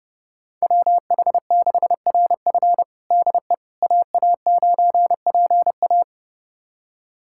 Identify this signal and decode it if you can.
Morse code, heard as a keyed tone, about 30 wpm, 700 Hz. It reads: WH6RF DE AA9PA
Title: USA Callsign DE Callsign - 5 Characters - 30wpm